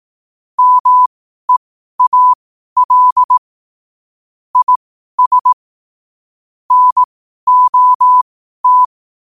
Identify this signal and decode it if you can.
Morse code, heard as a keyed tone, about 18 wpm, spaced as for 12 wpm, 1000 Hz. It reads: MEAL IS NOT
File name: qotdfiles / 12wpm